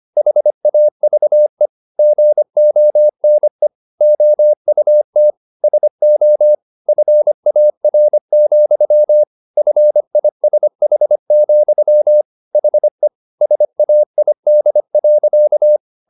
Morse code, heard as a keyed tone, about 25 wpm, 600 Hz. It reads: HAVE GONE OUT SO FAR, FISH, HE SAID.